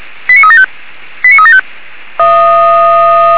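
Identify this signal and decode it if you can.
Five-tone selective call (ZVEI), heard as quick runs of five tones, tones of about 70 ms. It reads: 89375 89375
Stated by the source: ZVEI five-tone call 89374 received on an SDR by Feuerwehr Orlamünde.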